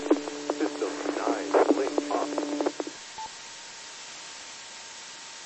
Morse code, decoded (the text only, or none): none